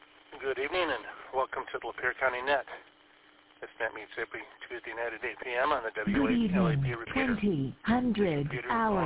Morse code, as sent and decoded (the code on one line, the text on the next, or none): none